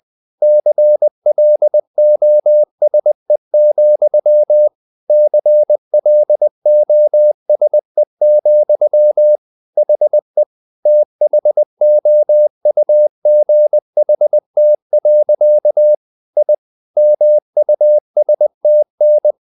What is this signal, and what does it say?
Morse code, heard as a keyed tone, about 20 wpm, 600 Hz. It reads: CLOSE, CLOSE, HE THOUGHT. I MUSTN
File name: the old man and the sea part 2 20wpm